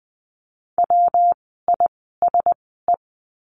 Morse code, heard as a keyed tone, about 20 wpm, spaced as for 15 wpm, 700 Hz. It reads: WISE
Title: 115 Lesson 19f - Letter w - Words 20wpm